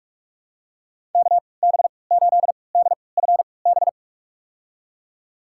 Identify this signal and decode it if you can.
Morse code, heard as a keyed tone, about 46 wpm, spaced as for 25 wpm, 700 Hz. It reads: KB8DFB
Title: USA Call Signs - 6 Characters - 3x Character Spacing - 45wpm